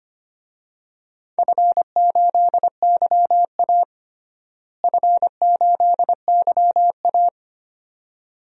Morse code, heard as a keyed tone, about 25 wpm, 700 Hz. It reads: F8YA F8YA
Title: DXCC-100 Call Signs - 4 Characters - Repeated 2x - 25wpm